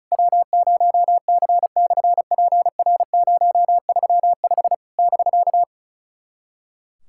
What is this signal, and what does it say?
Morse code, heard as a keyed tone, about 36 wpm, 690 Hz. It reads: W0C/PR035 <BK>
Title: SOTA QSO Practice 35wpm